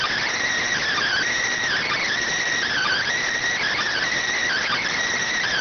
Emergency Alert System (SAME) tones, heard as harsh buzzing bursts. It is heard from 0.9 to 4.9 s.